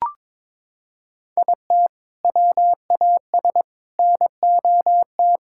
Morse code, heard as a keyed tone, about 22 wpm, 700 Hz. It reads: IT WAS NOT
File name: practice / 22wpm / Sentences with 2-Word Phrases - 22wpm